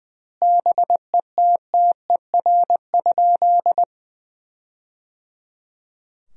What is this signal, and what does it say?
Morse code, heard as a keyed tone, about 20 wpm, 700 Hz. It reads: BETTER?
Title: 2x Word-Spacing - Sentences from Top 400 Words - 20wpm